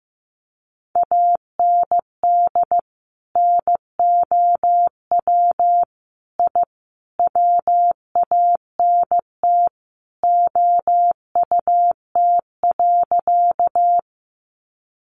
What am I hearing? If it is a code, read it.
Morse code, heard as a keyed tone, about 15 wpm, 700 Hz. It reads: AND NOW I WANT OUT.
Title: A Sound Of Thunder 15wpm